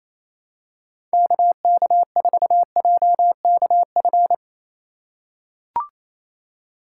Morse code, heard as a keyed tone, about 28 wpm, 700 Hz. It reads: KK4JKF
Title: USA Call Signs - 6 Characters - 3x Character Spacing - 28wpm